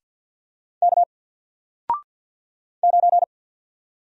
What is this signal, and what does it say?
Morse code, heard as a keyed tone, about 48 wpm, 700 Hz. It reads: K 9
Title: Single Character (Letter:Number:Punctuation:Prosign) Alt 50wpm